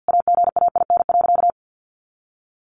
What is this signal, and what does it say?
Morse code, heard as a keyed tone, about 50 wpm, 700 Hz. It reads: AGAIN.
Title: Sentences from Top 1100 Words 50wpm